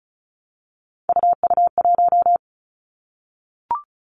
Morse code, heard as a keyed tone, about 36 wpm, 700 Hz. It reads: UU1
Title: Three Challenging Letter-Number Combinations 35wpm